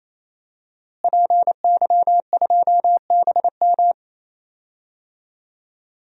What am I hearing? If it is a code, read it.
Morse code, heard as a keyed tone, about 28 wpm, 700 Hz. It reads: PY2BM
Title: DXCC-100 Call Signs - 5 Characters - Repeated 3x - 28wpm